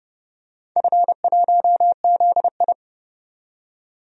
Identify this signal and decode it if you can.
Morse code, heard as a keyed tone, about 30 wpm, 700 Hz. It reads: F1ZI
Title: DXCC-100 Call Signs - 4 Characters - 30wpm